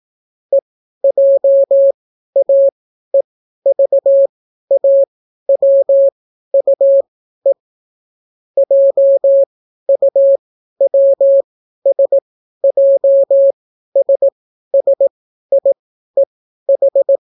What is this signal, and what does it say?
Morse code, heard as a keyed tone, about 18 wpm, spaced as for 12 wpm, 550 Hz. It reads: EJAEVAWUE JUWSJSSIEH